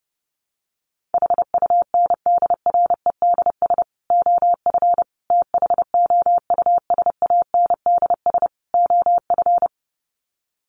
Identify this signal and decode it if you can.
Morse code, heard as a keyed tone, about 30 wpm, 700 Hz. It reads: HUNDREDS OF THOUSANDS OF
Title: Sets of 4 Words 30wpm